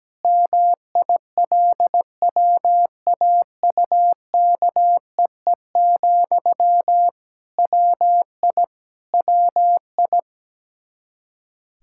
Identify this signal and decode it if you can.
Morse code, heard as a keyed tone, about 17 wpm, 700 Hz. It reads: MILWAUKEE, WI WI
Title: K1USN SST QSO Practice 17wpm